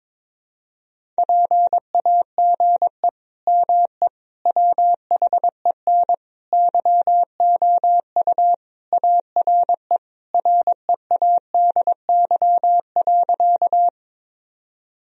Morse code, heard as a keyed tone, about 22 wpm, 700 Hz. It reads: PAGE ME WHEN YOU ARE READY.
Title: Sentences from Top 1000 Words - Review Part 1 - 22wpm